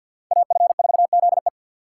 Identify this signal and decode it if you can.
Morse code, heard as a keyed tone, about 62 wpm, 700 Hz. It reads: AF4ZE
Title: Speed Racing - USA 2x2 Call Signs 50wpm